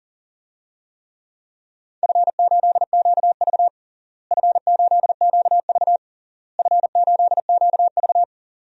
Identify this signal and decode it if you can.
Morse code, heard as a keyed tone, about 40 wpm, 700 Hz. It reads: F8QV F8QV F8QV
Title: DXCC-100 Call Signs - 4 Characters - Repeated 3x - 40wpm